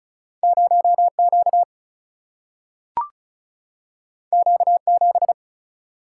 Morse code, heard as a keyed tone, about 36 wpm, 700 Hz. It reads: 0 Q Q 7
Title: Single-Letter Single-Number 35wpm